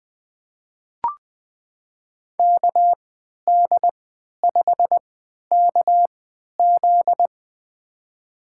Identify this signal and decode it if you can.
Morse code, heard as a keyed tone, about 20 wpm, spaced as for 11 wpm, 700 Hz. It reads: KD5KZ
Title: USA Call Signs - 5 Characters - 3x Character Spacing - 20wpm